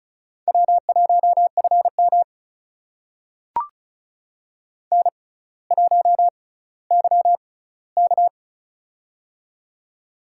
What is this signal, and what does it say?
Morse code, heard as a keyed tone, about 36 wpm, spaced as for 18 wpm, 700 Hz. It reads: W1FM N1YK
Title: USA Call Signs - 4 Characters - 6x Character Spacing - 35wpm